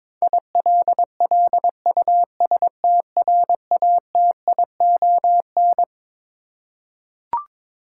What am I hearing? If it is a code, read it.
Morse code, heard as a keyed tone, about 22 wpm, 700 Hz. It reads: ILLUSTRATION
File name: Twelve Letter Words - 22wpm